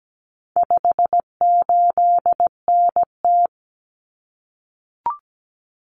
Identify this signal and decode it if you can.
Morse code, heard as a keyed tone, about 17 wpm, 700 Hz. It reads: 58NT
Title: Common Cut Numbers - 4-Digit 17wpm